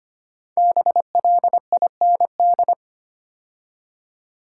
Morse code, heard as a keyed tone, about 25 wpm, 700 Hz. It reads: BLIND
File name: 5 Letter Words - First 3 Letters Given - 25wpm